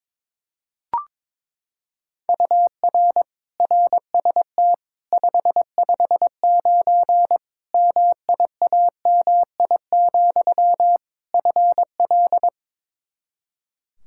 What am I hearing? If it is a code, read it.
Morse code, heard as a keyed tone, about 21 wpm, 700 Hz. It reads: UR RST 559 MIAMI, FL
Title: K1USN SST QSO Practice 22wpm